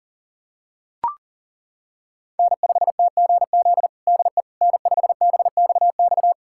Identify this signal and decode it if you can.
Morse code, heard as a keyed tone, about 40 wpm, 700 Hz. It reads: N5TGZ DE N5BXX
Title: USA Callsign DE Callsign - 5 Characters - 40wpm